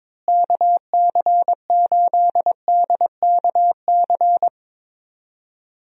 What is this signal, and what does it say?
Morse code, heard as a keyed tone, about 22 wpm, 700 Hz. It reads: KC8DKC
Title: USA Call Signs - 6 Characters - Repeated 3x - 22wpm